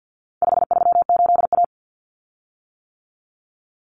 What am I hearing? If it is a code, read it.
Morse code, heard as a keyed tone, about 50 wpm, 700 Hz. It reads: 538A
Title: Common Cut Numbers - 4-Digit 50wpm